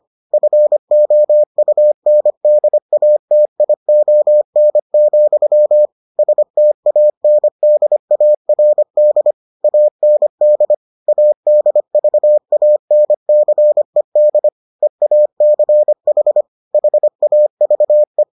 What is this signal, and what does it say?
Morse code, heard as a keyed tone, about 25 wpm, 600 Hz. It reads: FOUNDATION, STANDARD AND ADVANCED EACH HAVE